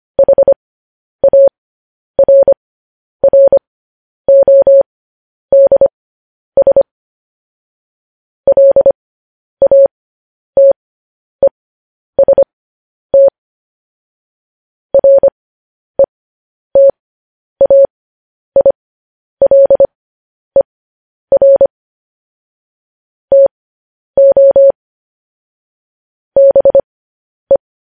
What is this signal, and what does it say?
Morse code, heard as a keyed tone, about 25 wpm, spaced as for 10 wpm, 550 Hz. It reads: HARRODS LATEST RETAILER TO BE
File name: podcast / News headlines-1x-25x10